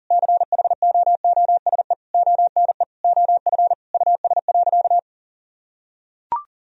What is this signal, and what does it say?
Morse code, heard as a keyed tone, about 40 wpm, 700 Hz. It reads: CHOOSE ONE OF US.
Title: Sentences from Top 1000 Words - Review Part 4 - 40wpm